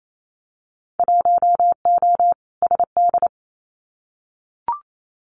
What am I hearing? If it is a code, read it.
Morse code, heard as a keyed tone, about 28 wpm, 700 Hz. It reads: 1O SD
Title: Winter Field Day Exchanges Encore 28wpm